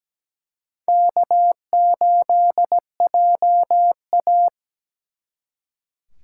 Morse code, heard as a keyed tone, about 20 wpm, 700 Hz. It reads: K8JA
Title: Speed Racing - USA 1x2 Call Signs 17wpm